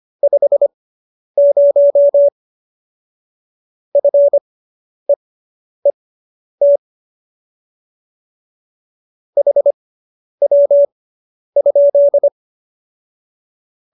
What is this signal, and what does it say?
Morse code, heard as a keyed tone, about 25 wpm, spaced as for 10 wpm, 570 Hz. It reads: 50 FEET HW?
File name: QSO220 10